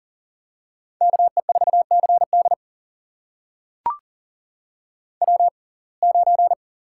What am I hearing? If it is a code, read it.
Morse code, heard as a keyed tone, about 40 wpm, spaced as for 29 wpm, 700 Hz. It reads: KE4CD W9
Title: USA Call Signs - 5 Characters - 6x Character Spacing - 40wpm